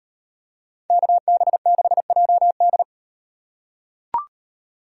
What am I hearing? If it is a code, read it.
Morse code, heard as a keyed tone, about 38 wpm, 700 Hz. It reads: KB6JD